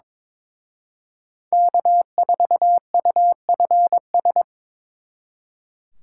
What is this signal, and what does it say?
Morse code, heard as a keyed tone, about 26 wpm, 700 Hz. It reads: K4UFS
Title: Speed Racing - USA 1x3 Call Signs 22wpm